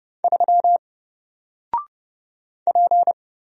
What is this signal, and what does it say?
Morse code, heard as a keyed tone, about 30 wpm, 700 Hz. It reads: 3 P
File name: Single-Letter-Number-30wpm